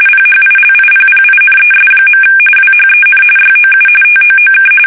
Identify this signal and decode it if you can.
RTTY, about 45.45 baud. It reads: PHILIPPINES. WHEN YOU TOOK
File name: fsk850 / battleshipnj